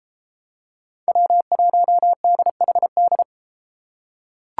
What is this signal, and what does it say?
Morse code, heard as a keyed tone, about 31 wpm, 700 Hz. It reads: W1DHD